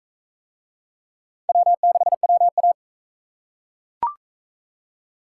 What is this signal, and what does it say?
Morse code, heard as a keyed tone, about 41 wpm, 700 Hz. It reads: W6WA